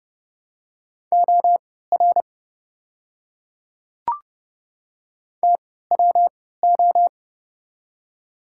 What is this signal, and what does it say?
Morse code, heard as a keyed tone, about 30 wpm, spaced as for 17 wpm, 700 Hz. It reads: OR TWO
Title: 3x Character Spacing - Top 100 Words - 30wpm